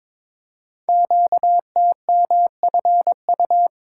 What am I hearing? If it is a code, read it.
Morse code, heard as a keyed tone, about 22 wpm, 700 Hz. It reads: QTMFU
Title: Code Groups - 5 Letters - 22wpm